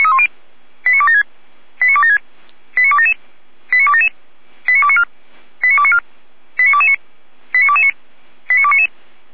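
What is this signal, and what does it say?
Five-tone selective call (ZVEI), heard as quick runs of five tones, tones of about 70 ms. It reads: 89276 89276 89277 89277 89294 89294 89229 89229 89299
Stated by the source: ZVEI five-tone call received by SDR.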